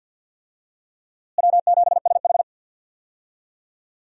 Morse code, heard as a keyed tone, about 52 wpm, 700 Hz. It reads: W7SH
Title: USA Call Signs - 4 Characters - Repeated 3x - 50wpm